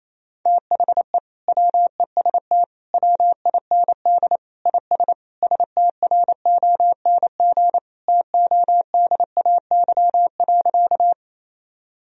Morse code, heard as a keyed tone, about 28 wpm, 700 Hz. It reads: THE WEST WIND IS STRONG TODAY.
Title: Sentences from Top 1000 Words - Review Part 2 - 28wpm